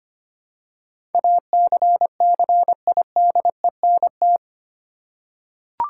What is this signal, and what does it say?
Morse code, heard as a keyed tone, about 25 wpm, 700 Hz. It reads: ACCIDENT
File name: 8 Letter Words - First 3 Letters Given - 25wpm